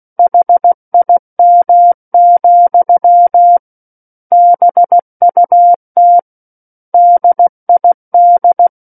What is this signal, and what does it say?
Morse code, heard as a keyed tone, about 16 wpm, 700 Hz. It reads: HIM, BUT DID
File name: Book One / 07-Chapter 7 - 16 WPM